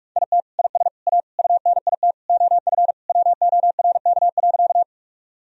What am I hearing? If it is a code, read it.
Morse code, heard as a keyed tone, about 46 wpm, 700 Hz. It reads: IT IS A UNIT OF WORK.
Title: Sentences from Top 1000 Words - Review Part 2 - 45wpm